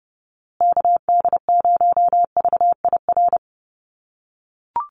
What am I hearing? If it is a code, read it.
Morse code, heard as a keyed tone, about 30 wpm, 700 Hz. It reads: KD0VIR
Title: USA Call Signs Repeated 3x 30wpm